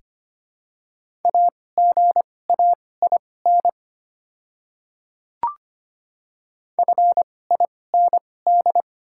Morse code, heard as a keyed tone, about 25 wpm, spaced as for 18 wpm, 700 Hz. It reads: AGAIN FIND
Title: 2x Character Spacing - Top 200 Words - 25wpm